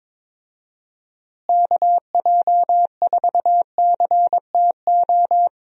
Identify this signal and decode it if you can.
Morse code, heard as a keyed tone, about 22 wpm, 700 Hz. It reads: KJ4CTO
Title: USA Call Signs - 6 Characters - Repeated 2x - 22wpm